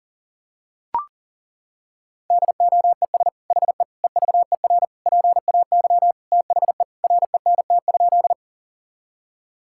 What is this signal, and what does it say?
Morse code, heard as a keyed tone, about 40 wpm, 700 Hz. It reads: DOES HE EVER PAY THE RENT?
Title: Sentences from Top 1000 Words - Review Part 5 - 40wpm